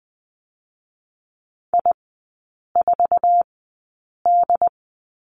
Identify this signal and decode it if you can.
Morse code, heard as a keyed tone, about 20 wpm, spaced as for 8 wpm, 700 Hz. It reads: I4D
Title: 087 Lesson 15e - Number 5 - Three Characters 20wpm